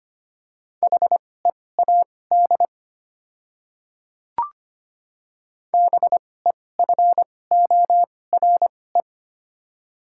Morse code, heard as a keyed tone, about 25 wpm, spaced as for 18 wpm, 690 Hz. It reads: HEAD BEFORE
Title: 2x Character Spacing - Top 500 Words - 25wpm